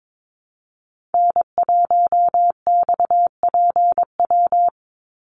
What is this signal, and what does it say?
Morse code, heard as a keyed tone, about 22 wpm, 700 Hz. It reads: N1XPW
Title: USA Call Signs 22wpm